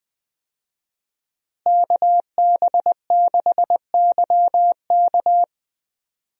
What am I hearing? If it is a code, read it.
Morse code, heard as a keyed tone, about 20 wpm, 700 Hz. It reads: KB6YK
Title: USA Call Signs - 5 Characters - Repeated 3x - 20wpm